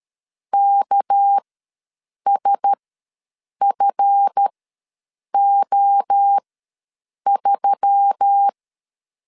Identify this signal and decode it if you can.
Morse code, heard as a keyed tone, about 13 wpm, spaced as for 7 wpm, 800 Hz. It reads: KSFO3
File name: LEZIONI / 7 GF9Z / 7 GF ws 12wpm 3140